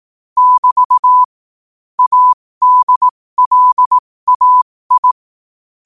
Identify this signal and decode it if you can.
Morse code, heard as a keyed tone, about 18 wpm, spaced as for 15 wpm, 1000 Hz. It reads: <BT> ADLAI
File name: qotdfiles / 15wpm